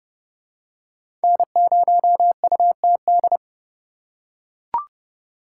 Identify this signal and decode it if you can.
Morse code, heard as a keyed tone, about 29 wpm, 700 Hz. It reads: N0UTD